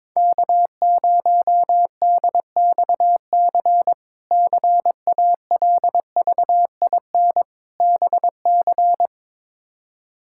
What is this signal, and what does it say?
Morse code, heard as a keyed tone, about 22 wpm, 700 Hz. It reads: K0DXC CALVIN BC
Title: K1USN SST QSO Practice - 22wpm